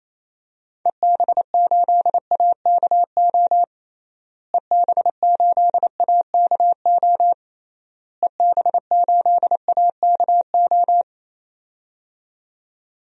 Morse code, heard as a keyed tone, about 27 wpm, 700 Hz. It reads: EB8AKO EB8AKO EB8AKO
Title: DXCC-100 Call Signs - 6 Characters - Repeated 3x - 28wpm